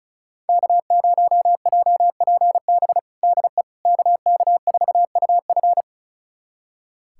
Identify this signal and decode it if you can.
Morse code, heard as a keyed tone, about 36 wpm, 700 Hz. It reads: K0JPB DE KK4UF
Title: USA Callsign DE Callsign - 5 Characters - 35wpm